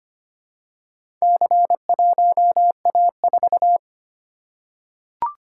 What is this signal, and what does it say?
Morse code, heard as a keyed tone, about 25 wpm, 700 Hz. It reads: C1A4
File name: practice / 25wpm / Four Letter-Number Combinations - 25wpm